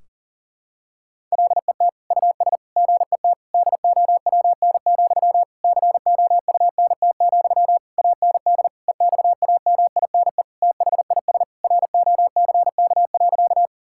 Morse code, heard as a keyed tone, about 40 wpm, 700 Hz. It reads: LET US GET DOWN, COUNT, AND EXAMINE THIS ROCK.
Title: Sentences from Top 1600 Words 40wpm